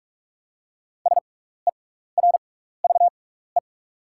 Morse code, heard as a keyed tone, about 44 wpm, spaced as for 15 wpm, 690 Hz. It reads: SERVE
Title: 6x Character Spacing - Top 500 Words - 45wpm